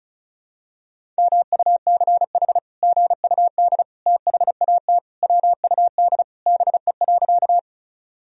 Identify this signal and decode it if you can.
Morse code, heard as a keyed tone, about 34 wpm, 700 Hz. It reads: MUCH GUD THAT WUD BE.
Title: Sentences from Top 200 Words and Abbreviations 35wpm